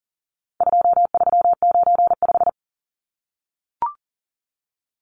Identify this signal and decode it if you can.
Morse code, heard as a keyed tone, about 40 wpm, 700 Hz. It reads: 2395
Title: Numbers - 4-Digit 40wpm